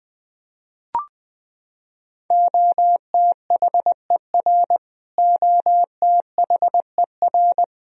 Morse code, heard as a keyed tone, about 20 wpm, 700 Hz. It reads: OTHER OTHER
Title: Top 100 Words Repeated 20wpm